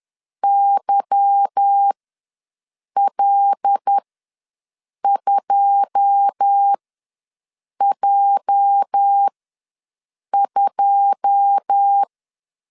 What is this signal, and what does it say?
Morse code, heard as a keyed tone, about 10 wpm, spaced as for 6 wpm, 800 Hz. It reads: YL2J2
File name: LEZIONI / 9 2 ws 10wpm 4130